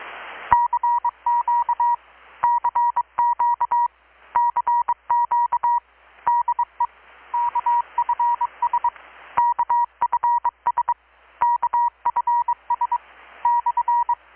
Morse code, heard as a keyed tone, about 23 wpm, 960 Hz. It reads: CQ CQ CQ DE KFS KFS KFS /
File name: blog / swl / 6-abril-2019-12695-USB-KPS